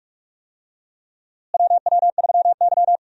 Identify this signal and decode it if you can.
Morse code, heard as a keyed tone, about 43 wpm, 700 Hz. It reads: WW3Y
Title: USA Call Signs - 4 Characters - Repeated 3x - 45wpm